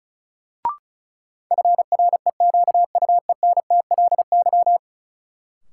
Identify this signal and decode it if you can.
Morse code, heard as a keyed tone, about 34 wpm, 700 Hz. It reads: FREQUENTLY
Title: Sentences from Top 1300 Words - New Word Drill - 35wpm